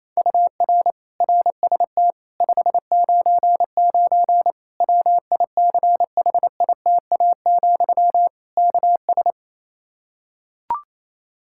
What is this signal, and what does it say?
Morse code, heard as a keyed tone, about 28 wpm, 700 Hz. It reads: UR RST 599 WICHITA, KS
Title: Practice QSO - Exchange 1 - Beginning - 28wpm